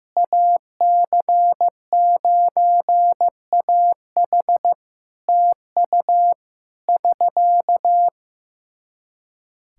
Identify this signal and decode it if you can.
Morse code, heard as a keyed tone, about 15 wpm, 700 Hz. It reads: AC9AH TU <SK>
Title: Practice QSO - Exchange 4 - Ending - 15wpm